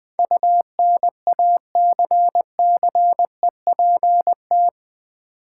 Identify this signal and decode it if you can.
Morse code, heard as a keyed tone, about 20 wpm, 700 Hz. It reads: UNACCEPT
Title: Build A Word - 4 Letters At A Time - 12-Letter Words - 20wpm